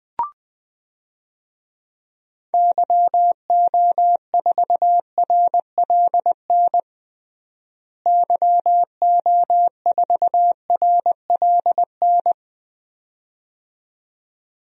Morse code, heard as a keyed tone, about 21 wpm, 700 Hz. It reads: YO4RLN YO4RLN
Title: DXCC-100 Call Signs - 6 Characters - Repeated 2x - 20wpm